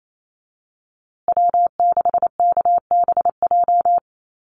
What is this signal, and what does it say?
Morse code, heard as a keyed tone, about 28 wpm, 700 Hz. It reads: W6KBJ
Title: USA Call Signs Repeated 3x 28wpm